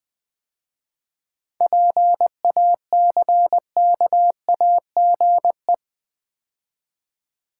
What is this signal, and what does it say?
Morse code, heard as a keyed tone, about 20 wpm, 690 Hz. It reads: PACKAGE
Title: 7 Letter Words - First 3 Letters Given - 20wpm